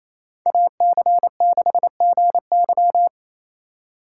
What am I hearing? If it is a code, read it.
Morse code, heard as a keyed tone, about 33 wpm, 700 Hz. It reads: AC6GY
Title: Speed Racing - USA 2x2 Call Signs 28wpm